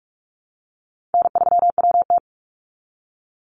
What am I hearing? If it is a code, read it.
Morse code, heard as a keyed tone, about 44 wpm, 700 Hz. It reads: N3WT
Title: Call Signs Repeated 3x - 45wpm